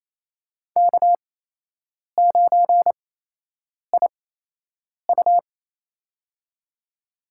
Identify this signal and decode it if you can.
Morse code, heard as a keyed tone, about 28 wpm, spaced as for 8 wpm, 700 Hz. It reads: K9IU
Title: USA Call Signs - 4 Characters - 8x Character Spacing - 28wpm